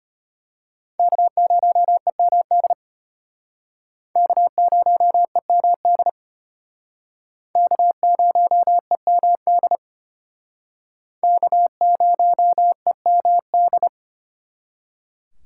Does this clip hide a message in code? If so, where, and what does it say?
Yes: Morse code from 1.0 to 13.9 s, heard as a keyed tone, about 32 wpm, 700 Hz. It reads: K0EMD K0EMD K0EMD K0EMD